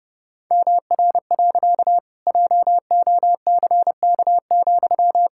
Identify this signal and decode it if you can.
Morse code, heard as a keyed tone, about 30 wpm, 700 Hz. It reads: MR. JOCK,
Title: Pangram Challenge v2 - 30wpm